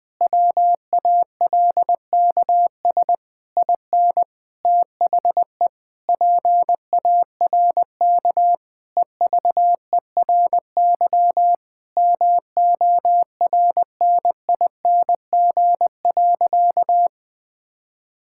Morse code, heard as a keyed tone, about 20 wpm, 700 Hz. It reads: WALKS IN THE PARK EVERY MORNING.
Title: Sentences with 2-Word Phrases - Spoken Beginning - 20wpm